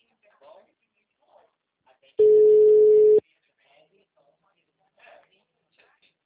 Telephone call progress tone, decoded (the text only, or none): ringback tone